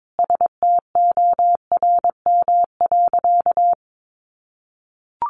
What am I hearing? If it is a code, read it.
Morse code, heard as a keyed tone, about 22 wpm, 700 Hz. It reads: STORM.
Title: Sentences from Top 1500 Words 22wpm